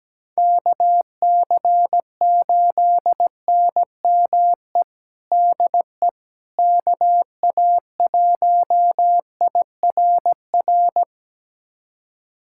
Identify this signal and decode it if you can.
Morse code, heard as a keyed tone, about 17 wpm, 700 Hz. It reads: KC8NME DE KA1IRR
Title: USA Callsign DE Callsign - 6 Characters - 17wpm